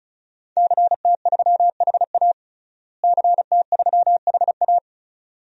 Morse code, heard as a keyed tone, about 34 wpm, 700 Hz. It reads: CT3HA CT3HA
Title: DXCC-100 Call Signs - 5 Characters - Repeated 2x - 35wpm